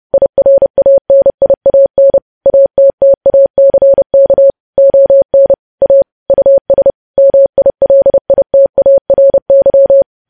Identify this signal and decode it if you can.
Morse code, heard as a keyed tone, about 30 wpm, 550 Hz. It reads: IRANIAN ATTACK ON A US MILITARY
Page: news headlines in morse code at 30 wpm